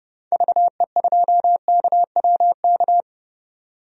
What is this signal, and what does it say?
Morse code, heard as a keyed tone, about 30 wpm, 700 Hz. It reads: VE2KWK
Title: DXCC-100 Call Signs - 6 Characters - 30wpm